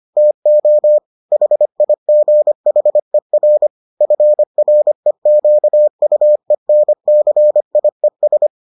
Morse code, heard as a keyed tone, about 25 wpm, 600 Hz. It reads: TO HIGHER FREQUENCIES